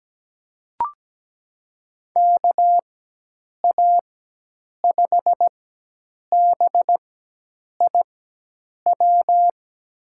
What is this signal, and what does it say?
Morse code, heard as a keyed tone, about 17 wpm, spaced as for 8 wpm, 700 Hz. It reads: KA5BIW
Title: USA Call Signs - 6 Characters - 4x Character Spacing - 17wpm